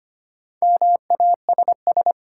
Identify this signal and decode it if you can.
Morse code, heard as a keyed tone, about 25 wpm, 700 Hz. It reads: MASS
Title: Sets of 5 Words Encore - New Word Drill - 25wpm